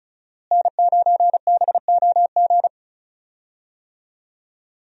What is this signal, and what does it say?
Morse code, heard as a keyed tone, about 36 wpm, 700 Hz. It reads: N9BOG
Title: USA Call Signs - 5 Characters - Repeated 3x - 35wpm